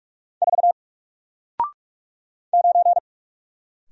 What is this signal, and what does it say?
Morse code, heard as a keyed tone, about 44 wpm, 700 Hz. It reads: 4 9
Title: Single Letter-Number Alt 45wpm